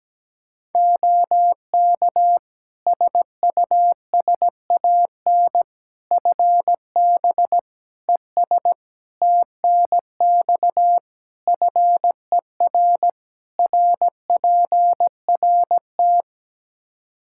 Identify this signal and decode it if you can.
Morse code, heard as a keyed tone, about 17 wpm, 700 Hz. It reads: OK SUSAN FB ES TNX FER RPRT
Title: Practice QSO - Exchange 2 - Info - 17wpm